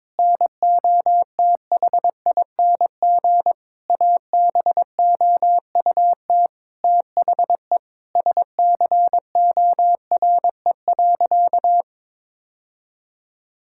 Morse code, heard as a keyed tone, about 22 wpm, 700 Hz. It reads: NOTHING ABOUT THE SCORE.
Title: Sentences from Top 1600 Words 22wpm